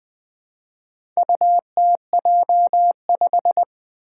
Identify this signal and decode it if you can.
Morse code, heard as a keyed tone, about 20 wpm, 700 Hz. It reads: UTJ5
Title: Four Challenging Letter-Number Combinations 20wpm